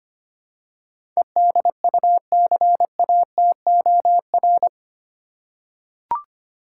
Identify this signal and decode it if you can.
Morse code, heard as a keyed tone, about 25 wpm, 700 Hz. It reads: EDUCATOR
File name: Eight Letter Words - 25wpm